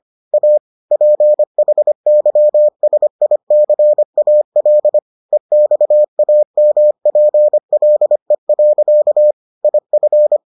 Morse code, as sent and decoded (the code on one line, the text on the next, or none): .- / .--. .... -.-- ... .. -.-. .- .-.. / . -..- .- -- .--. .-.. .-.-.- / .. ..-.
A PHYSICAL EXAMPLE. IF